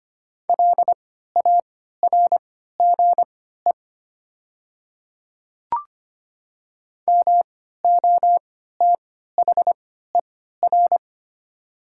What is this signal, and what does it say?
Morse code, heard as a keyed tone, about 25 wpm, spaced as for 14 wpm, 700 Hz. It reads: LARGE MOTHER
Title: 3x Character Spacing - Top 200 Words - 25wpm